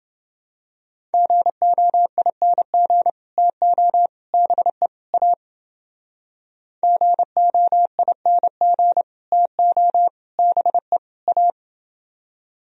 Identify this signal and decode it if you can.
Morse code, heard as a keyed tone, about 29 wpm, 700 Hz. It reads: GOING TO BE A GOING TO BE A